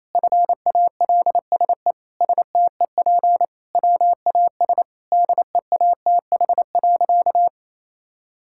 Morse code, heard as a keyed tone, about 27 wpm, 700 Hz. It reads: FALSE STEP WAS DEATH.
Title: Sentences from Top 1600 Words 28wpm